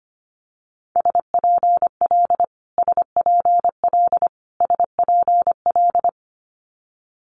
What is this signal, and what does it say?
Morse code, heard as a keyed tone, about 25 wpm, 700 Hz. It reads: SPL SPL SPL
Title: Multi-letter Phonemes Repeated 3x 25wpm